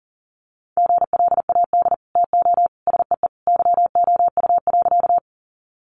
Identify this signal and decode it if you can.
Morse code, heard as a keyed tone, about 40 wpm, 700 Hz. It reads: GLAD TO SEE YOU.
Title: Sentences from Top 800 Words 40wpm